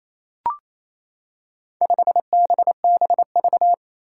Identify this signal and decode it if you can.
Morse code, heard as a keyed tone, about 28 wpm, 700 Hz. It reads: 5BBV
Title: Four Challenging Letter-Number Combinations 28wpm